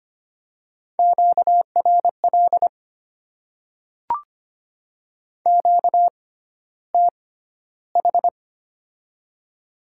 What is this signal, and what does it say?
Morse code, heard as a keyed tone, about 25 wpm, spaced as for 13 wpm, 700 Hz. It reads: QRL QTH
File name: Top 20 QSO Elements - 6x Character Spacing - 25wpm